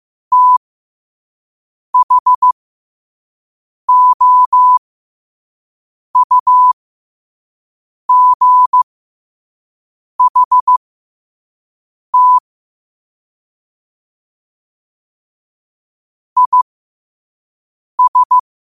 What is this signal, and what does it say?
Morse code, heard as a keyed tone, about 15 wpm, spaced as for 5 wpm, 1000 Hz. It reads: THOUGHT IS
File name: qotdfiles / 5wpm